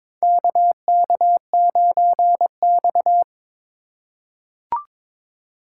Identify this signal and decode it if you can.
Morse code, heard as a keyed tone, about 22 wpm, 700 Hz. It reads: KK9X
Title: USA Call Signs - 4 Characters - 3x Character Spacing - 22wpm